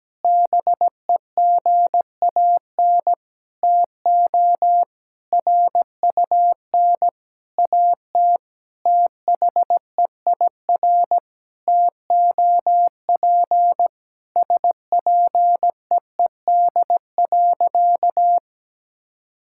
Morse code, heard as a keyed tone, about 17 wpm, 700 Hz. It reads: BEGAN TO RUN AT THEIR TOP SPEED.
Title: Sentences from Top 1000 Words - Review Part 5 - 17wpm